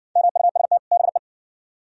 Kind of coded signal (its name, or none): Morse code